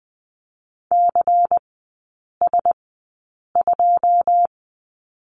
Morse code, heard as a keyed tone, about 20 wpm, spaced as for 8 wpm, 700 Hz. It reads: CS2
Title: 138 Lesson 23e - Letter f - Three Characters 20wpm